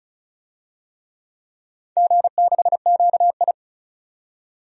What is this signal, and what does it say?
Morse code, heard as a keyed tone, about 36 wpm, 700 Hz. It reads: G6QI
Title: DXCC-100 Call Signs - 4 Characters - 35wpm